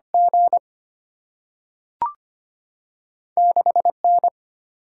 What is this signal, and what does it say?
Morse code, heard as a keyed tone, about 25 wpm, 700 Hz. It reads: G 6N
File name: Two Letter-Number Combinations - 25wpm